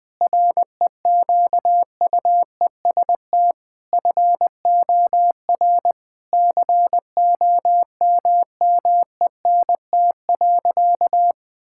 Morse code, heard as a keyed tone, about 20 wpm, 700 Hz. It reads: REQUEST FOR COMMENT.
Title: Sentences with 8-Word Phrases - Spoken Beginning - 20wpm